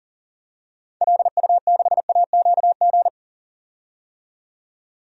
Morse code, heard as a keyed tone, about 38 wpm, 700 Hz. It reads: LU6AQG
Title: DXCC-100 Call Signs - 6 Characters - 40wpm